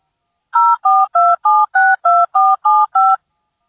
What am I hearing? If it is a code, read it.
DTMF, heard as telephone touch tones, tones of about 220 ms. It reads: 042762475